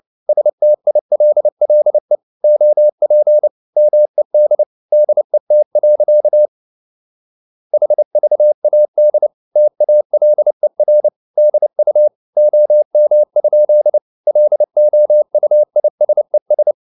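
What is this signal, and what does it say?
Morse code, heard as a keyed tone, about 29 wpm, 600 Hz. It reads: STILLE OP MED DET. HVAD TALER DU OM? LOUISES
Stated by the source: Fast (QRQ) Morse at 29 wpm, from CW med Gnister.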